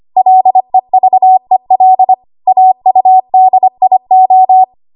Morse code, heard as a keyed tone, about 25 wpm, 750 Hz. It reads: LEVEL AUDIO
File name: Archive / 25 WPM / 25WPM